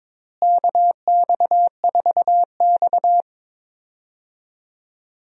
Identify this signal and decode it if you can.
Morse code, heard as a keyed tone, about 22 wpm, 700 Hz. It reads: KX4X
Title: USA Call Signs - 4 Characters - Repeated 3x - 22wpm